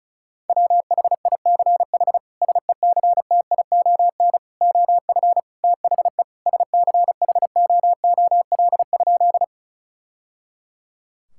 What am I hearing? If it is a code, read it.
Morse code, heard as a keyed tone, about 36 wpm, 700 Hz. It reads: WHICH SECTION OF THE SCHOOL?
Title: Sentences from Top 1000 Words - Review Part 5 - 35wpm